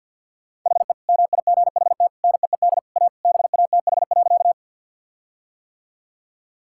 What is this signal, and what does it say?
Morse code, heard as a keyed tone, about 50 wpm, 700 Hz. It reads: HE MIGHT NEED A BATH.
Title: Sentences from Top 1600 Words 50wpm